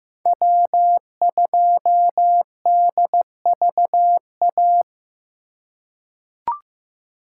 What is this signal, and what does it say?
Morse code, heard as a keyed tone, about 15 wpm, 700 Hz. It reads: W2DVA
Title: USA Call Signs - 5 Characters - 2x Character Spacing - 15wpm